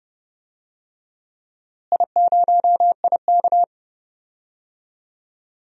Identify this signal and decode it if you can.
Morse code, heard as a keyed tone, about 30 wpm, 700 Hz. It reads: I0IK
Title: DXCC-100 Call Signs - 4 Characters - Repeated 2x - 30wpm